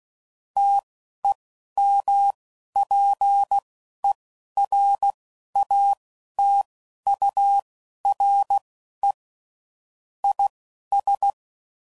Morse code, heard as a keyed tone, about 16 wpm, spaced as for 11 wpm, 800 Hz. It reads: TEMPERATURE IS
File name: CW Code Groups @10WPM / Weather Sets / Weather 3 17